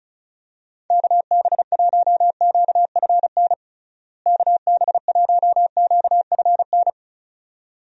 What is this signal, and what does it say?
Morse code, heard as a keyed tone, about 36 wpm, 700 Hz. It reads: KB1QFN KB1QFN
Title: USA Call Signs - 6 Characters - Repeated 2x - 35wpm